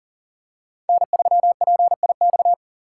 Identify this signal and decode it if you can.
Morse code, heard as a keyed tone, about 38 wpm, 700 Hz. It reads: N3PIX
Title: USA Call Signs - 5 Characters - 6x Character Spacing - 40wpm